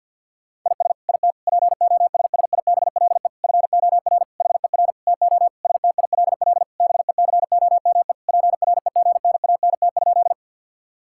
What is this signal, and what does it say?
Morse code, heard as a keyed tone, about 50 wpm, 700 Hz. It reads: IS IT POSSIBLE FOR HER TO STILL BECOME PREGNANT?
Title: Sentences from Top 1000 Words - Review Part 6 - 50wpm